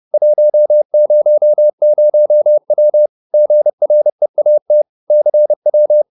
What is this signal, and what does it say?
Morse code, heard as a keyed tone, about 30 wpm, 600 Hz. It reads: 100W GREAT CW